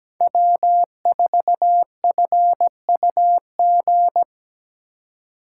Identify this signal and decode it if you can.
Morse code, heard as a keyed tone, about 17 wpm, 700 Hz. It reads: W4FUG
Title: USA Call Signs - 5 Characters - 3x Character Spacing - 17wpm